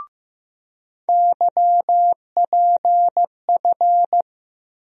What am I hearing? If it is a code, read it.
Morse code, heard as a keyed tone, about 15 wpm, 700 Hz. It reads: YPF
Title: USA Call Sign Postfixes - 3 Letters 15wpm